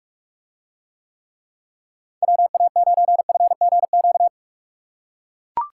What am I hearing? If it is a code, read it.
Morse code, heard as a keyed tone, about 44 wpm, 700 Hz. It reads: WA9FGQ